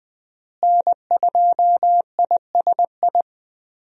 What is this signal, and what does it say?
Morse code, heard as a keyed tone, about 20 wpm, 700 Hz. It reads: N2ISI
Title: USA Call Signs - 5 Characters - Repeated 3x - 20wpm